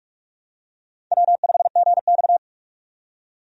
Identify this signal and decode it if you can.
Morse code, heard as a keyed tone, about 46 wpm, 700 Hz. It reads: W5GX
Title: USA Call Signs - 4 Characters - 45wpm